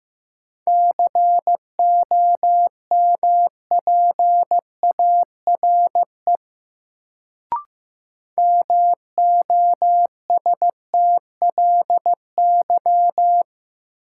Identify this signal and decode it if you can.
Morse code, heard as a keyed tone, about 15 wpm, 700 Hz. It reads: COMPARE MOSTLY
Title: Sentences from Top 1300 Words - New Word Drill - 15wpm